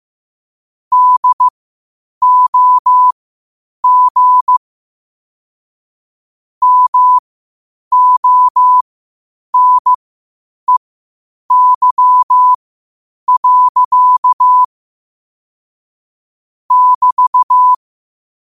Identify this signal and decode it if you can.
Morse code, heard as a keyed tone, about 15 wpm, spaced as for 8 wpm, 1000 Hz. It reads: DOG MONEY. <BT>
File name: qotdfiles / 8wpm